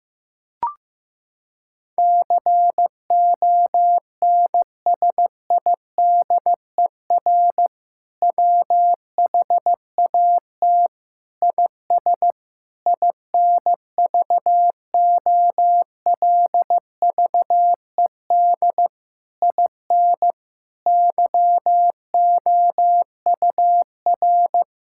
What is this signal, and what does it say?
Morse code, heard as a keyed tone, about 15 wpm, 700 Hz. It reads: CONSIDER WHAT IS INVOLVED IN YOUR
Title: Sentences from Top 1000 Words - Review Part 5 - 15wpm